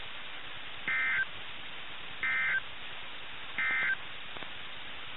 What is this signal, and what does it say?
Emergency Alert System (SAME) tones, heard as harsh buzzing bursts. It is heard from 0.8 to 4.0 s.
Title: NOAA Weather Radio - KIH61